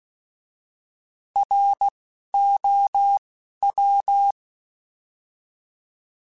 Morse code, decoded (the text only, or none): R O W